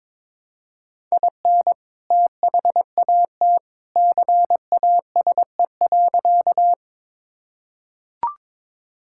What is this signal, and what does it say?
Morse code, heard as a keyed tone, about 22 wpm, 700 Hz. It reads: IN THAT CASE.
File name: practice / 22wpm / Sentences with 2-Word Phrases - 22wpm